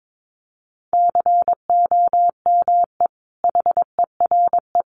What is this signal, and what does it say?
Morse code, heard as a keyed tone, about 22 wpm, 700 Hz. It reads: COME HERE
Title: Sentences from Top 800 Words 22wpm